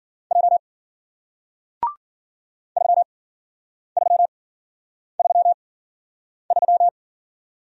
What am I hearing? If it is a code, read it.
Morse code, heard as a keyed tone, about 55 wpm, spaced as for 9 wpm, 700 Hz. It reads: J 3333